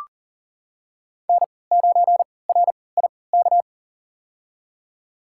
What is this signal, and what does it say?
Morse code, heard as a keyed tone, about 41 wpm, spaced as for 23 wpm, 700 Hz. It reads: N9RIK
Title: USA Call Signs - 3x Character Spacing - 40wpm